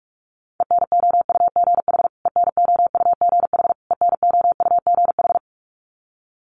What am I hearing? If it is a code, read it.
Morse code, heard as a keyed tone, about 44 wpm, 700 Hz. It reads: ENOUGH ENOUGH ENOUGH
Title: Top 500 Words Repeated 45wpm